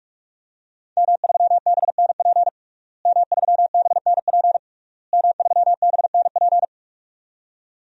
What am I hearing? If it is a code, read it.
Morse code, heard as a keyed tone, about 43 wpm, 700 Hz. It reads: M3BNP M3BNP M3BNP
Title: DXCC-100 Call Signs - 5 Characters - Repeated 3x - 45wpm